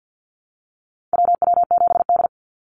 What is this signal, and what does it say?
Morse code, heard as a keyed tone, about 48 wpm, 700 Hz. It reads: WW7D
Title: NA SOTA Activator Callsigns 50wpm